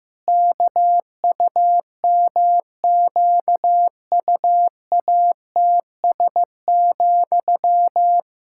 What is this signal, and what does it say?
Morse code, heard as a keyed tone, about 15 wpm, 700 Hz. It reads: KUMQUATS,
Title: Pangram Challenge v2 - 15wpm